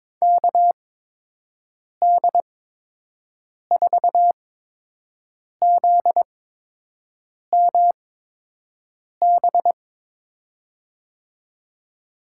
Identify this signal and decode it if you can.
Morse code, heard as a keyed tone, about 22 wpm, spaced as for 6 wpm, 700 Hz. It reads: KD4ZMB
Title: USA Call Signs - 6 Characters - 8x Character Spacing - 22wpm